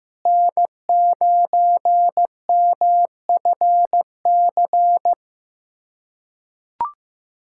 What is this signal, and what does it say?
Morse code, heard as a keyed tone, about 15 wpm, 700 Hz. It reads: N9MFC
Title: USA Callsign DE Callsign - 5 Characters - 15wpm